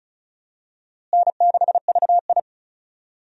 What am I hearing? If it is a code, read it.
Morse code, heard as a keyed tone, about 36 wpm, 700 Hz. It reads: N6VI
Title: USA Call Signs - 4 Characters - Repeated 2x - 35wpm